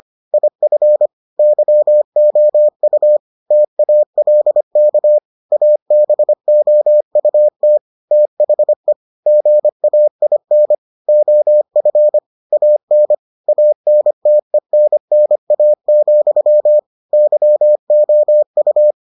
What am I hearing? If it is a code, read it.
Morse code, heard as a keyed tone, about 25 wpm, 600 Hz. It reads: IF YOU TALK ABOUT THE GAIN OF AN ANTENNA, YOU